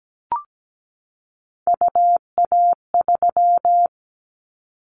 Morse code, heard as a keyed tone, about 17 wpm, 700 Hz. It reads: UA3
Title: Callsign Prefixes for Top 100 DXCC - 17wpm